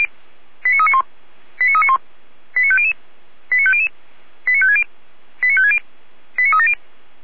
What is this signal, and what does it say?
Five-tone selective call (ZVEI), heard as quick runs of five tones, tones of about 70 ms. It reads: 89391 89391 89500 89500 89560 89560 89370